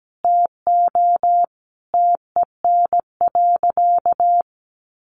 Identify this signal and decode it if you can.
Morse code, heard as a keyed tone, about 17 wpm, 700 Hz. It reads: TO TEN.
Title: Sentences from Top 1300 Words 17wpm